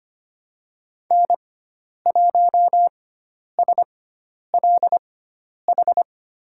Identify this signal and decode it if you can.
Morse code, heard as a keyed tone, about 25 wpm, spaced as for 10 wpm, 700 Hz. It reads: N1SLH
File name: Lesson 16g - Letter c - Call Signs 25wpm